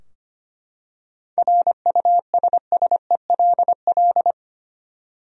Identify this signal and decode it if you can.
Morse code, heard as a keyed tone, about 25 wpm, 700 Hz. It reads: RUSSELL
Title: USA Common Names - 25wpm